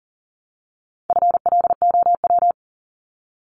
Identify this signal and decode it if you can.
Morse code, heard as a keyed tone, about 40 wpm, 700 Hz. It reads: FLOW
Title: Top 1001-1250 Words 40wpm